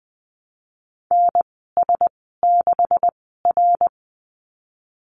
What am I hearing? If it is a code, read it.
Morse code, heard as a keyed tone, about 20 wpm, spaced as for 15 wpm, 700 Hz. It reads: NS6R
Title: 161 Lesson 26g - Letter g - Call Signs 20wpm